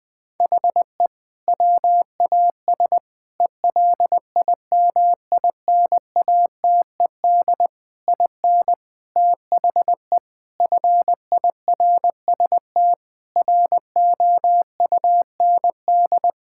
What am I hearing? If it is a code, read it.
Morse code, heard as a keyed tone, about 21 wpm, 700 Hz. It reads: HE WAS ELIMINATED IN THE FIRST ROUND
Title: Sentences with 6-Word Phrases - 20wpm